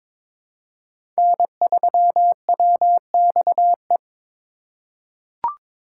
Morse code, heard as a keyed tone, about 21 wpm, 700 Hz. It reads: N3WXE
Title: USA Call Signs - 5 Characters - 4x Character Spacing - 22wpm